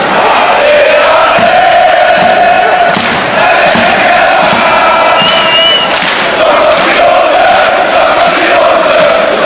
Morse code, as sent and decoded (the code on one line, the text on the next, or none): none